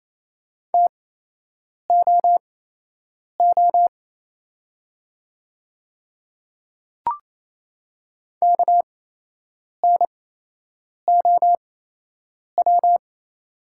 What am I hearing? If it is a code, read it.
Morse code, heard as a keyed tone, about 28 wpm, spaced as for 8 wpm, 700 Hz. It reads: TOO KNOW
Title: 8x Character Spacing - Top 200 Words - 28wpm